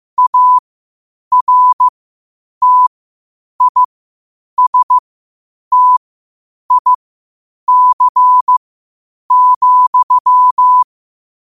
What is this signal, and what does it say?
Morse code, heard as a keyed tone, about 15 wpm, spaced as for 8 wpm, 1000 Hz. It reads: ARTISTIC,
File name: qotdfiles / 8wpm